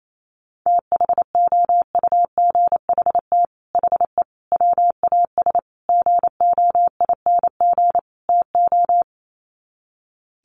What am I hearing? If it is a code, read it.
Morse code, heard as a keyed tone, about 28 wpm, 700 Hz. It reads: THOUGHT HE WAS GOING TO
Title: Sets of 5 Words Encore2 28wpm